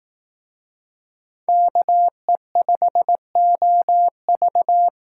Morse code, heard as a keyed tone, about 18 wpm, 700 Hz. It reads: KE5OV
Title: Speed Racing - USA 2x2 Call Signs 15wpm